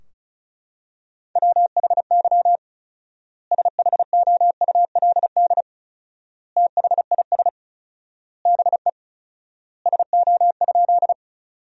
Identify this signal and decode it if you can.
Morse code, heard as a keyed tone, about 34 wpm, 700 Hz. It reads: WHY SHOULD THIS BE SO?
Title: 4x Word-Spacing - Sentences from Top 300 Words - 35wpm